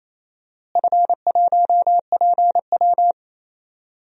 Morse code, heard as a keyed tone, about 28 wpm, 700 Hz. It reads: F1PW
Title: DXCC-100 Call Signs - 4 Characters - Repeated 2x - 28wpm